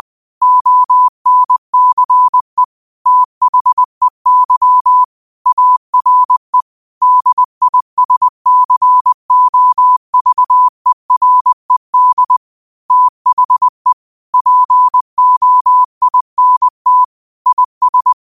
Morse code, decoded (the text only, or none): ONCE THEY ARE DISCOVERED THE POINT IS